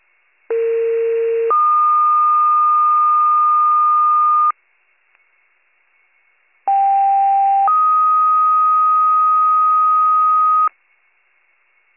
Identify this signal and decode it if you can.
Two-tone page, heard as a one-second tone, then a three-second tone, about 460 then 1220 Hz, about 790 then 1250 Hz. It is heard from 0.5 to 10.7 s.